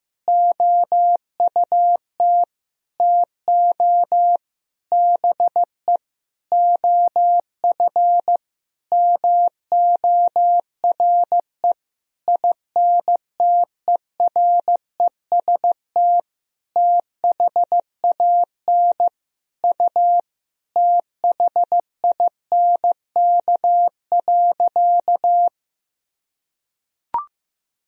Morse code, heard as a keyed tone, about 14 wpm, 700 Hz. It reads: OUT TO BE OF MORE INTEREST THAN U THINK.
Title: Sentences from Top 400 Words and Abbreviations 15wpm